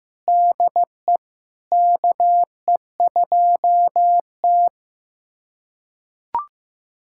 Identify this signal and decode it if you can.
Morse code, heard as a keyed tone, about 16 wpm, 700 Hz. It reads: DE KE2T
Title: USA Callsign DE Callsign - 4 Characters - 15wpm